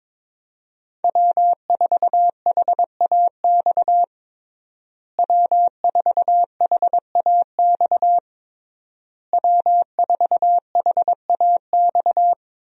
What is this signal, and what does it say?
Morse code, heard as a keyed tone, about 22 wpm, 700 Hz. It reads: W4HAX W4HAX W4HAX
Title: USA Call Signs - 5 Characters - Repeated 3x - 22wpm